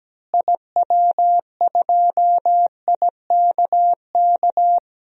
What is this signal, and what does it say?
Morse code, heard as a keyed tone, about 17 wpm, 700 Hz. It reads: IW2IKK
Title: DXCC-100 Call Signs - 6 Characters - Repeated 2x - 17wpm